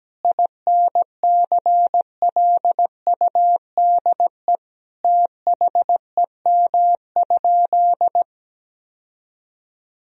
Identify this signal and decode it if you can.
Morse code, heard as a keyed tone, about 17 wpm, 700 Hz. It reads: INCLUDE THEM?
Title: Sentences from Top 1000 Words - Review Part 3 - 17wpm